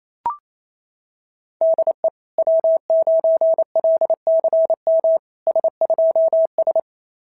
Morse code, heard as a keyed tone, about 28 wpm, 660 Hz. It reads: DE W9LCM S2S
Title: SOTA QSO Practice 28wpm